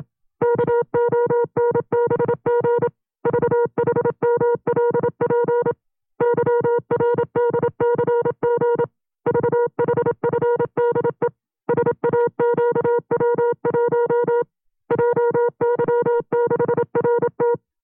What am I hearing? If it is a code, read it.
Morse code, heard as a keyed tone, about 27 wpm, 480 Hz. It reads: KONBG VHMLP YRDCG VHFDE SAQW1 JY6RT
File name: CW-practice/25WPM